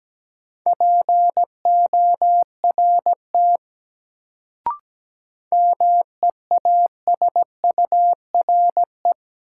Morse code, heard as a keyed tone, about 17 wpm, 700 Hz. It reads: PORT MEASURE
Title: Sentences from Top 1000 Words - Review Part 1 - New Word Drill - 17wpm